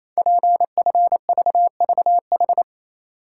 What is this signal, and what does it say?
Morse code, heard as a keyed tone, about 28 wpm, 690 Hz. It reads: PFVVH
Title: Code Groups - 5 Letters - 28wpm